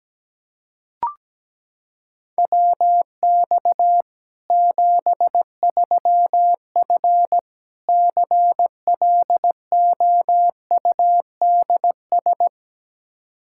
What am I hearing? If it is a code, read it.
Morse code, heard as a keyed tone, about 17 wpm, 700 Hz. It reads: WX 73F CLOUDS
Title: Practice QSO - Exchange 2 - Info - 17wpm